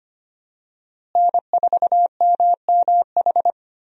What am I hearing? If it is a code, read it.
Morse code, heard as a keyed tone, about 25 wpm, 700 Hz. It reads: N4MMH
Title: USA Call Signs - 5 Characters - Repeated 3x - 25wpm